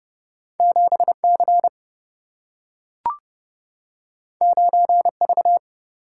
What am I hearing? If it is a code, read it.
Morse code, heard as a keyed tone, about 30 wpm, 700 Hz. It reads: 7 C 9 V
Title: Single-Letter Single-Number 30wpm